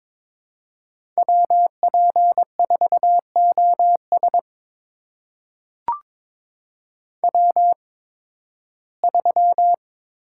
Morse code, heard as a keyed tone, about 22 wpm, spaced as for 14 wpm, 700 Hz. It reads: WP4OS W3
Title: USA Call Signs - 5 Characters - 8x Character Spacing - 22wpm